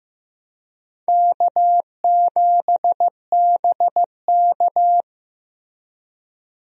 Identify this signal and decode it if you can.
Morse code, heard as a keyed tone, about 15 wpm, 700 Hz. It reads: K7BK
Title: USA Call Signs - 4 Characters - Repeated 3x - 15wpm